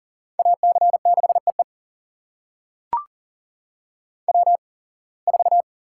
Morse code, heard as a keyed tone, about 38 wpm, spaced as for 26 wpm, 700 Hz. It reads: AC6EE W4
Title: USA Call Signs - 5 Characters - 8x Character Spacing - 40wpm